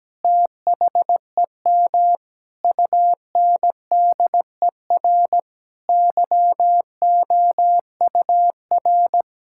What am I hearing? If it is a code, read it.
Morse code, heard as a keyed tone, about 17 wpm, 700 Hz. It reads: THEM UNDER YOUR
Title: Sentences from Top 1600 Words 17wpm